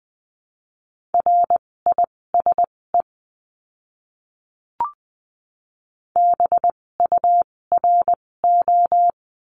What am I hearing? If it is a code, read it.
Morse code, heard as a keyed tone, about 20 wpm, spaced as for 16 wpm, 700 Hz. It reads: RISE BURO
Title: 256 Lesson 42a - Words 12-20wpm